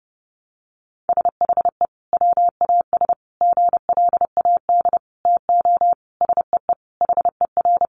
Morse code, heard as a keyed tone, about 30 wpm, 700 Hz. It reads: SHE WAS GLAD TO SEE HER
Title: Sentences from Top 800 Words 30wpm